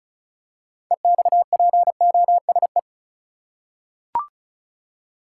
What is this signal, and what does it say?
Morse code, heard as a keyed tone, about 36 wpm, 700 Hz. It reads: EXPOSE
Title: Six Letter Words - 35wpm